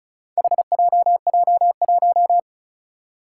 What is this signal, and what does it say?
Morse code, heard as a keyed tone, about 36 wpm, 700 Hz. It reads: HJJ1
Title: Four Challenging Letter-Number Combinations 35wpm